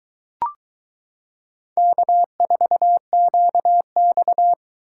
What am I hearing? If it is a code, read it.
Morse code, heard as a keyed tone, about 22 wpm, 700 Hz. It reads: K4QX